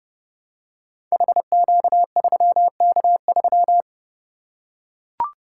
Morse code, heard as a keyed tone, about 29 wpm, 700 Hz. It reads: HQ3K3
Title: Code Groups - 5 Letter:Numbers - 30wpm